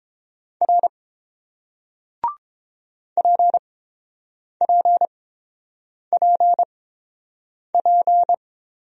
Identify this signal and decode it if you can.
Morse code, heard as a keyed tone, about 30 wpm, spaced as for 7 wpm, 700 Hz. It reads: R PPPP